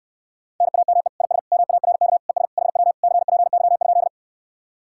Morse code, heard as a keyed tone, about 74 wpm, 700 Hz. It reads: NAME IS MARK ES HW CPY?